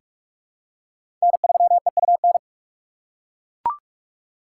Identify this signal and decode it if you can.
Morse code, heard as a keyed tone, about 41 wpm, 700 Hz. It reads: N3EUN
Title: USA Call Signs - 5 Characters - 8x Character Spacing - 45wpm